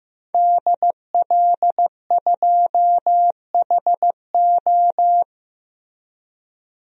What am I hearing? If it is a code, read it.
Morse code, heard as a keyed tone, about 15 wpm, 700 Hz. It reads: DL2HO
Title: DXCC-100 Call Signs - 5 Characters - Repeated 2x - 15wpm